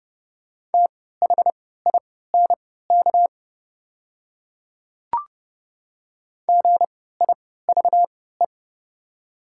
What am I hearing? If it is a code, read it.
Morse code, heard as a keyed tone, about 30 wpm, spaced as for 17 wpm, 690 Hz. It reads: THINK GIVE
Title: 3x Character Spacing - Top 500 Words - 30wpm